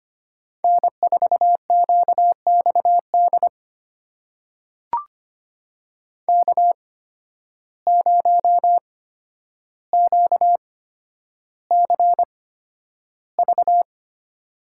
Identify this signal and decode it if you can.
Morse code, heard as a keyed tone, about 25 wpm, spaced as for 8 wpm, 700 Hz. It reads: N4QX DK0QCV
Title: USA Call Signs - 8x Character Spacing - 25wpm